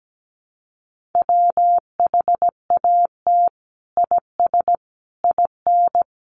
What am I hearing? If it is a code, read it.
Morse code, heard as a keyed tone, about 17 wpm, 700 Hz. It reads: WHAT IS IN
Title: Sentences from Top 200 Words 17wpm